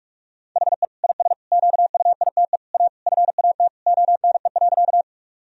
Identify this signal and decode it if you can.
Morse code, heard as a keyed tone, about 46 wpm, 700 Hz. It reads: HE IS QUITE A FAT ONE.
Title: Sentences from Top 1000 Words - Review Part 4 - 45wpm